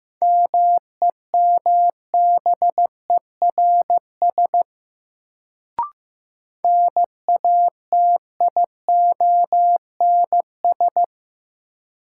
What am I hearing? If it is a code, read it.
Morse code, heard as a keyed tone, about 15 wpm, 700 Hz. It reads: MEMBERS NATIONS
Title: Sets of 3 Words - New Word Drill - 15wpm